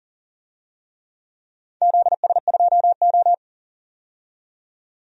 Morse code, heard as a keyed tone, about 40 wpm, 700 Hz. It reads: ZS2O
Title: DXCC-100 Call Signs - 4 Characters - 40wpm